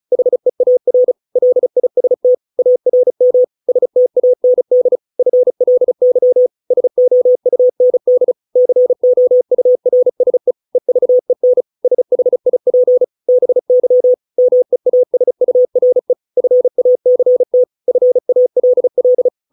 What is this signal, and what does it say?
Morse code, heard as a keyed tone, about 34 wpm, 500 Hz. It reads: HEAR LIST ARM STAND FLY SOUND COURSE EVEN SHIP BY MEASURE FACT FALL